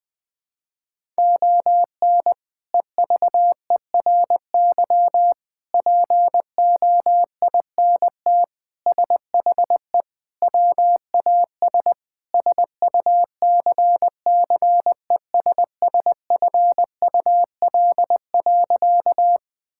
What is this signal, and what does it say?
Morse code, heard as a keyed tone, about 20 wpm, 700 Hz. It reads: ON EVERY POINT SHE WAS SUCCESSFUL.
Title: Sentences from Top 1000 Words - Review Part 6 - 20wpm